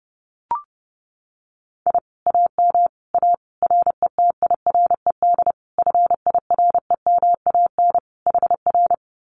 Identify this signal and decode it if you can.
Morse code, heard as a keyed tone, about 30 wpm, 700 Hz. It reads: I AM A RETIRED FIREMAN HR
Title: Common QSO Phrases Encore 30wpm